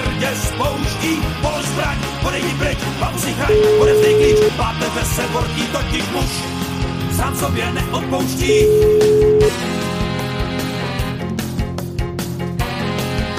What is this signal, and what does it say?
Ringback tone (Europe, 425 Hz), with speech.